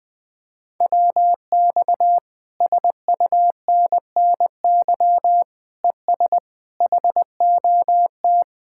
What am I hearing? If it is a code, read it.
Morse code, heard as a keyed tone, about 20 wpm, 700 Hz. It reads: WX SUNNY ES HOT
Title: Speed-Racing Common QSO Phrases Encore - 20wpm